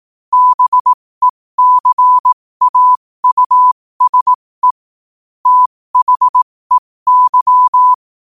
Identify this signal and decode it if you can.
Morse code, heard as a keyed tone, about 18 wpm, spaced as for 15 wpm, 1000 Hz. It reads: BECAUSE THEY
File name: qotdfiles / 15wpm